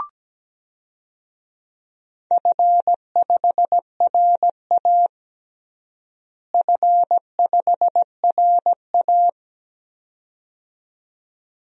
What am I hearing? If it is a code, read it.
Morse code, heard as a keyed tone, about 17 wpm, 700 Hz. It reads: F5RA F5RA
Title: DXCC-100 Call Signs - 4 Characters - Repeated 2x - 17wpm